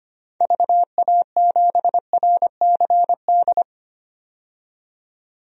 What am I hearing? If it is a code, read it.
Morse code, heard as a keyed tone, about 25 wpm, 700 Hz. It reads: VA7RCD
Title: DXCC-100 Call Signs - 6 Characters - 25wpm